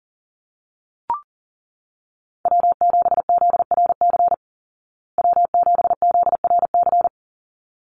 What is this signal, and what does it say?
Morse code, heard as a keyed tone, about 40 wpm, 700 Hz. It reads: W7ZRC W7ZRC
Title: Call Signs Repeated 3x - 40wpm